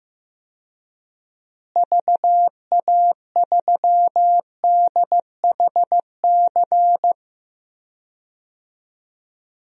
Morse code, heard as a keyed tone, about 15 wpm, 700 Hz. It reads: VA3DHC
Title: DXCC-100 Call Signs - 6 Characters - Repeated 2x - 15wpm